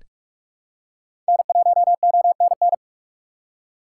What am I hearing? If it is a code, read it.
Morse code, heard as a keyed tone, about 45 wpm, 690 Hz. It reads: N1ONN
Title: USA Call Signs - 5 Characters - 2x Character Spacing - 45wpm